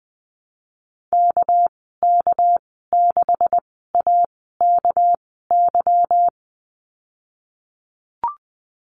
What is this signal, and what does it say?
Morse code, heard as a keyed tone, about 20 wpm, spaced as for 15 wpm, 700 Hz. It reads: KK6AKY
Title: 200 Lesson 32g - Letter k - Call Signs 20wpm